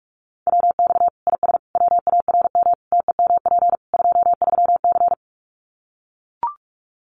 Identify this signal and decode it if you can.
Morse code, heard as a keyed tone, about 44 wpm, 700 Hz. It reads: WX IS WARM TEMP 23C
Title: Common QSO Phrases 45wpm